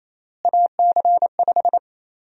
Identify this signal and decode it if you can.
Morse code, heard as a keyed tone, about 28 wpm, 700 Hz. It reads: AC5
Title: USA Call Sign Prefixes - Repeated 2x - 28wpm